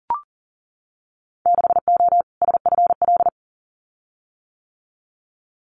Morse code, heard as a keyed tone, about 42 wpm, 700 Hz. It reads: 6O SFL
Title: Winter Field Day Exchanges Encore 40wpm